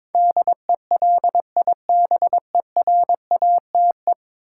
Morse code, heard as a keyed tone, about 22 wpm, 700 Hz. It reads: DELIBERATE
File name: practice / 22wpm / 10 Letter Words - First 3 Letters Given - 22wpm